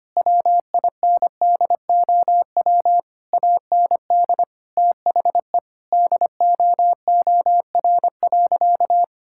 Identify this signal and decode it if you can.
Morse code, heard as a keyed tone, about 25 wpm, 700 Hz. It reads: WINDOW AND THE DOOR.
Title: Sentences from Top 1600 Words 25wpm